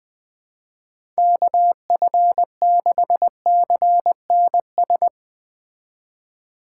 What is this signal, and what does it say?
Morse code, heard as a keyed tone, about 20 wpm, 700 Hz. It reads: KF6CNS
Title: Speed Racing - USA 2x3 Call Signs 17wpm